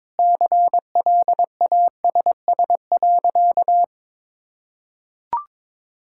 Morse code, heard as a keyed tone, about 21 wpm, 700 Hz. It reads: CLASS.
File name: practice / 22wpm / Easy Sentences with Hints - 22wpm